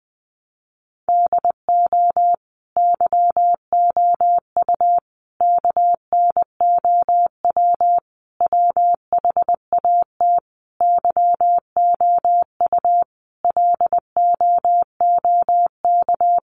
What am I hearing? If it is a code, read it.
Morse code, heard as a keyed tone, about 20 wpm, 700 Hz. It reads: DO YOU KNOW WHAT YOU LOOK
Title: Sentences from Top 100 Words 20wpm